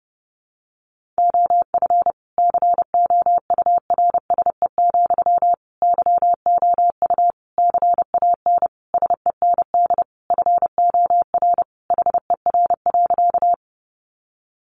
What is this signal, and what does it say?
Morse code, heard as a keyed tone, about 30 wpm, 700 Hz. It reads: OF COURSE, YOU CAN SEND FOR HER.
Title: Sentences from Top 1500 Words 30wpm